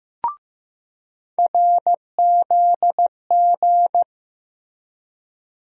Morse code, heard as a keyed tone, about 15 wpm, 700 Hz. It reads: RZG
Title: USA Call Sign Postfixes - 3 Letters 15wpm